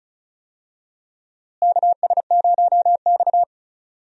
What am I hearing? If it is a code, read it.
Morse code, heard as a keyed tone, about 34 wpm, 700 Hz. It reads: KS0X
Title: USA Call Signs - 4 Characters - Repeated 2x - 35wpm